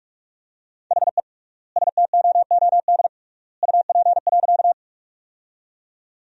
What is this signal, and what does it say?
Morse code, heard as a keyed tone, about 41 wpm, 700 Hz. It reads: HE STOOD UP.
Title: 3x Word-Spacing - Sentences from Top 500 Words - 45wpm